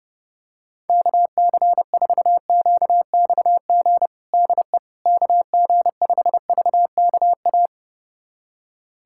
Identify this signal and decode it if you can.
Morse code, heard as a keyed tone, about 29 wpm, 700 Hz. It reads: KC4QXG DE KG5VKA
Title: USA Callsign DE Callsign - 6 Characters - 30wpm